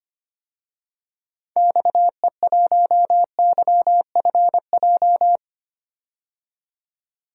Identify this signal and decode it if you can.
Morse code, heard as a keyed tone, about 25 wpm, 700 Hz. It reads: XE1YFJ
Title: DXCC-100 Call Signs - 6 Characters - 25wpm